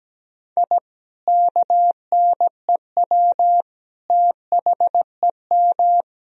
Morse code, heard as a keyed tone, about 17 wpm, 700 Hz. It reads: I KNEW THEM
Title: Sentences from Top 1000 Words - Review Part 1 - 17wpm